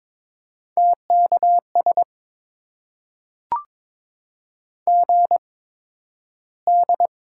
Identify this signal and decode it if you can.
Morse code, heard as a keyed tone, about 22 wpm, spaced as for 12 wpm, 700 Hz. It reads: TKS GD
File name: Top 90 QSO Elements - 8x Character Spacing - 22wpm